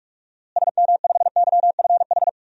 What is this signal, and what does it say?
Morse code, heard as a keyed tone, about 46 wpm, 700 Hz. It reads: SM5YFH
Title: DXCC-100 Call Signs - 6 Characters - Repeated 2x - 45wpm